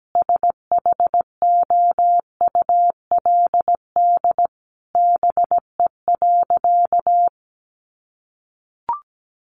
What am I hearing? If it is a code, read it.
Morse code, heard as a keyed tone, about 17 wpm, 700 Hz. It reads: SHOULD BE.
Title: Sentences from Top 700 Words 17wpm